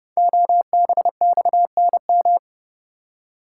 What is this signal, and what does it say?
Morse code, heard as a keyed tone, about 29 wpm, 700 Hz. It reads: OBXNM
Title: Code Groups - 5 Letters - 30wpm